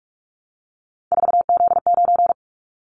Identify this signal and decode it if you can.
Morse code, heard as a keyed tone, about 46 wpm, 700 Hz. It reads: 4Z9
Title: Callsign Prefixes for Top 50 DXCC - 45wpm